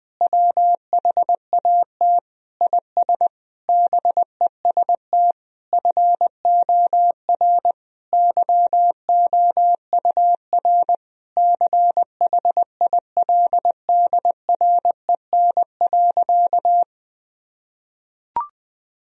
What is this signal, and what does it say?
Morse code, heard as a keyed tone, about 20 wpm, 700 Hz. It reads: WHAT IS BEST FOR YOUR CHILDREN.
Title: Sentences from Top 1000 Words - Review Part 4 - 20wpm